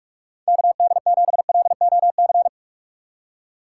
Morse code, heard as a keyed tone, about 44 wpm, 700 Hz. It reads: KD7LOC
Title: USA Call Signs - 6 Characters - 6x Character Spacing - 45wpm